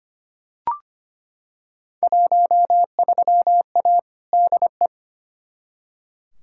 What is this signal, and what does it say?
Morse code, heard as a keyed tone, about 25 wpm, 700 Hz. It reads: 13A DE
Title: Field Day Exchanges Encore - 25wpm